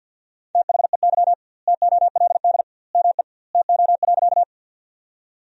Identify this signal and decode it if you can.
Morse code, heard as a keyed tone, about 50 wpm, 700 Hz. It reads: THEY TOLD ME TO.
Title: 2x Word-Spacing - Sentences from Top 400 Words - 50wpm